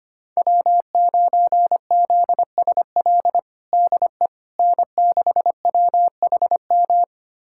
Morse code, heard as a keyed tone, about 25 wpm, 700 Hz. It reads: W9ZSL DE N6WHM
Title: USA Callsign DE Callsign - 5 Characters - 25wpm